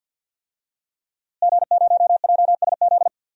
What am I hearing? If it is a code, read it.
Morse code, heard as a keyed tone, about 50 wpm, 700 Hz. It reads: G0JSZ
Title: DXCC-100 Call Signs - 5 Characters - Repeated 2x - 50wpm